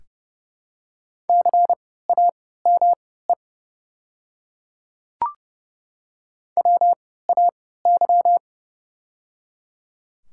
Morse code, heard as a keyed tone, about 30 wpm, spaced as for 17 wpm, 700 Hz. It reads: CAME WAY
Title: 3x Character Spacing - Top 200 Words - 30wpm